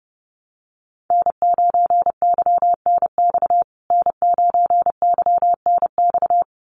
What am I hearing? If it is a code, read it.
Morse code, heard as a keyed tone, about 28 wpm, 700 Hz. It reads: N9YNX N9YNX
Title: USA Call Signs Repeated 2x 30wpm